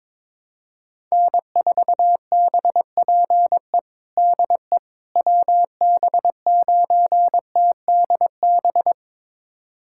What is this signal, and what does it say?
Morse code, heard as a keyed tone, about 21 wpm, 700 Hz. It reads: N4BPE DE WB9TDB
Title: USA Callsign DE Callsign 22wpm